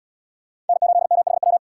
Morse code, heard as a keyed tone, about 64 wpm, 700 Hz. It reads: N8MHW